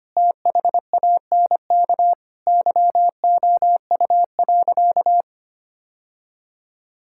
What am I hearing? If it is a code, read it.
Morse code, heard as a keyed tone, about 25 wpm, 700 Hz. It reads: THANK YOU.
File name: practice / 25wpm / Sentences with 2-Word Phrases - 25wpm